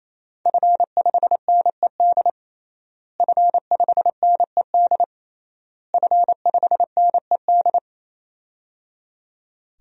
Morse code, heard as a keyed tone, about 28 wpm, 700 Hz. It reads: F5NED F5NED F5NED
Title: DXCC-100 Call Signs - 5 Characters - Repeated 3x - 28wpm